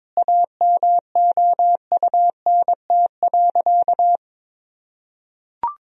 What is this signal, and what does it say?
Morse code, heard as a keyed tone, about 22 wpm, 700 Hz. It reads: AMOUNT.
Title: Sentences from Top 1000 Words - Review Part 4 - 22wpm